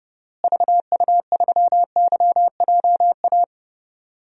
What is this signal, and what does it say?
Morse code, heard as a keyed tone, about 32 wpm, 700 Hz. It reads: VU3YJA
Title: DXCC-100 Call Signs - 6 Characters - Repeated 2x - 30wpm